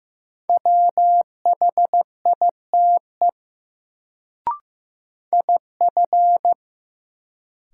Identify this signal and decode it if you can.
Morse code, heard as a keyed tone, about 15 wpm, 700 Hz. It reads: WHITE IF
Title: Sets of 2 Words - New Word Drill - 15wpm